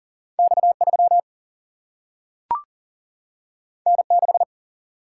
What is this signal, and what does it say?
Morse code, heard as a keyed tone, about 40 wpm, 700 Hz. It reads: X 3 N 6
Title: Single-Letter Single-Number 40wpm